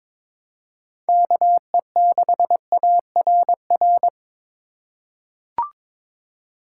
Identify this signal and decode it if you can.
Morse code, heard as a keyed tone, about 21 wpm, 700 Hz. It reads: KE6ARR
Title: USA Call Signs - 6 Characters - 2x Character Spacing - 22wpm